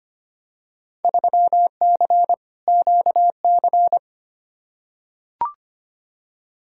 Morse code, heard as a keyed tone, about 25 wpm, 700 Hz. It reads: 3C QC
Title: Field Day Exchanges Encore - 25wpm